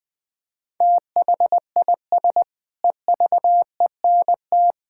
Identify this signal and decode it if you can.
Morse code, heard as a keyed tone, about 20 wpm, 700 Hz. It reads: THIS EVENT
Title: Sentences with 7-Word Phrases - Spoken Beginning - 20wpm